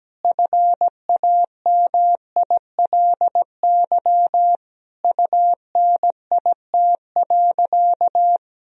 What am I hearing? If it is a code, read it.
Morse code, heard as a keyed tone, about 17 wpm, 700 Hz. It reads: FAMILY UNIT.
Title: Sentences from Top 1000 Words - Review Part 2 - 17wpm